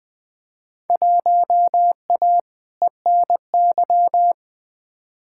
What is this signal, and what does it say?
Morse code, heard as a keyed tone, about 21 wpm, 700 Hz. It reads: 1A ENY
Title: Field Day Exchanges Encore - 20wpm